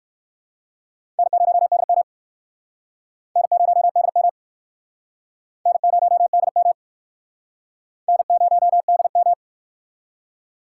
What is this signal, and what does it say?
Morse code, heard as a keyed tone, about 58 wpm, 700 Hz. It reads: N0DM N0DM N0DM N0DM